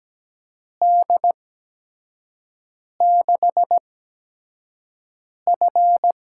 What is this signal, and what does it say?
Morse code, heard as a keyed tone, about 17 wpm, spaced as for 5 wpm, 700 Hz. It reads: D6F
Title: USA Call Signs - 4 Characters - 8x Character Spacing - 17wpm